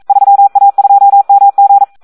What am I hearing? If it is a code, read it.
Morse code, heard as a keyed tone, about 42 wpm, 800 Hz. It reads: 3A2MG
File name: Audio practica CW / SD 1 42